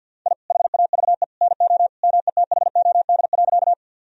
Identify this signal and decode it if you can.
Morse code, heard as a keyed tone, about 50 wpm, 700 Hz. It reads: I HAVE NO METHOD.
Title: Sentences from Top 1000 Words - Review Part 3 - 50wpm